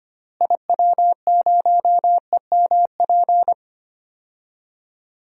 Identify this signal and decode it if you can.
Morse code, heard as a keyed tone, about 25 wpm, 700 Hz. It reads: IW0EMP
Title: DXCC-100 Call Signs - 6 Characters - Repeated 3x - 25wpm